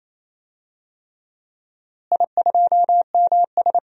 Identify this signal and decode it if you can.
Morse code, heard as a keyed tone, about 28 wpm, 700 Hz. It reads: I2MS
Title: DXCC-100 Call Signs - 4 Characters - Repeated 3x - 28wpm